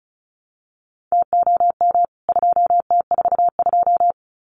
Morse code, heard as a keyed tone, about 34 wpm, 700 Hz. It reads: TOM 2T42
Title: CWT Exchange 35wpm